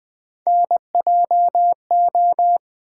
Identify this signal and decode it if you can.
Morse code, heard as a keyed tone, about 21 wpm, 700 Hz. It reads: NJO
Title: Three Letter-Number Combinations - 20wpm